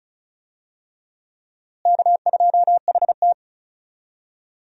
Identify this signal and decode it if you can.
Morse code, heard as a keyed tone, about 34 wpm, 700 Hz. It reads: K2HT
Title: USA Call Signs - 4 Characters - Repeated 3x - 35wpm